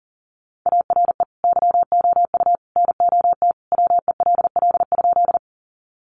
Morse code, heard as a keyed tone, about 40 wpm, 700 Hz. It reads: ARE YOU NOT WELL?
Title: Sentences from Top 600 Words 40wpm